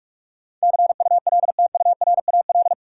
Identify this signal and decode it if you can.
Morse code, heard as a keyed tone, about 46 wpm, 700 Hz. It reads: CULTURAL
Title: Sentences from Top 1500 Words - New Word Drill - 45wpm